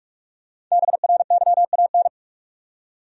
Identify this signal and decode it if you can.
Morse code, heard as a keyed tone, about 46 wpm, 700 Hz. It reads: BRYAN
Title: USA Common Names - 45wpm